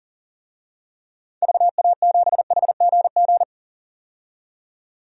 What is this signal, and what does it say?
Morse code, heard as a keyed tone, about 40 wpm, 700 Hz. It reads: VA7HGG
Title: DXCC-100 Call Signs - 6 Characters - 40wpm